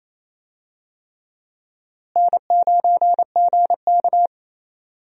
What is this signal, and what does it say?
Morse code, heard as a keyed tone, about 27 wpm, 700 Hz. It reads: N9GK
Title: USA Call Signs - 4 Characters - Repeated 3x - 28wpm